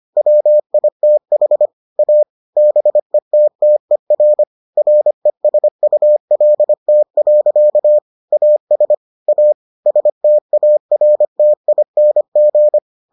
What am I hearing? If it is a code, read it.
Morse code, heard as a keyed tone, about 25 wpm, 600 Hz. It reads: WITH A BETTER RESULT. AS A STARTING